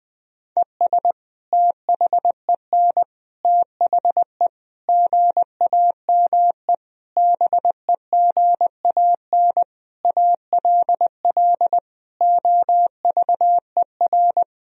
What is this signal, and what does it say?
Morse code, heard as a keyed tone, about 20 wpm, 700 Hz. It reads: ES THEN THE GAME BEGAN ALL OVER
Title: Sentences from Top 500 Words and Abbreviations 20wpm